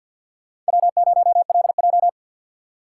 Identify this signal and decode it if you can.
Morse code, heard as a keyed tone, about 61 wpm, 700 Hz. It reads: W0LJ
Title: Speed Racing - USA 1x2 Call Signs 50wpm